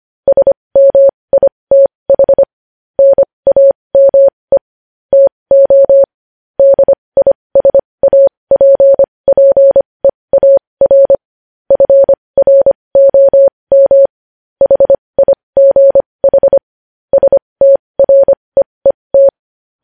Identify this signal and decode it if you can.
Morse code, heard as a keyed tone, about 25 wpm, spaced as for 20 wpm, 550 Hz. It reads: SMITH NAME TO DISAPPEAR FROM HIGH STREET
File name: podcast / News headlines-1x-25x20